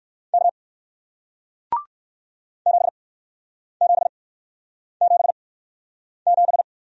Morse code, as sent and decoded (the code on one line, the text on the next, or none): -.- / --... --... --... --...
K 7777